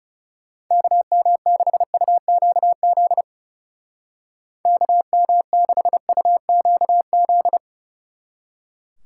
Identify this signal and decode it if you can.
Morse code, heard as a keyed tone, about 35 wpm, 700 Hz. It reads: KM6UQZ KM6UQZ
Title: Speed Racing - USA 2x3 Call Signs 30wpm